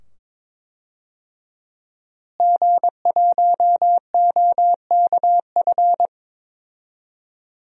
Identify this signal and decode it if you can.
Morse code, heard as a keyed tone, about 22 wpm, 700 Hz. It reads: G1OKF
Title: DXCC-100 Call Signs - 5 Characters - Repeated 2x - 22wpm